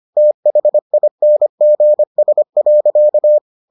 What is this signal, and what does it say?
Morse code, heard as a keyed tone, about 25 wpm, 600 Hz. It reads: THINGS.